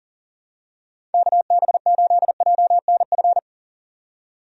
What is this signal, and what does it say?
Morse code, heard as a keyed tone, about 40 wpm, 700 Hz. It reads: KB8JNF
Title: USA Call Signs - 6 Characters - 40wpm